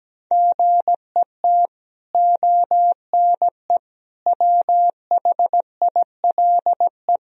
Morse code, heard as a keyed tone, about 17 wpm, 700 Hz. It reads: GET ONE WHILE
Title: Sentences from Top 1000 Words - Review Part 1 - 17wpm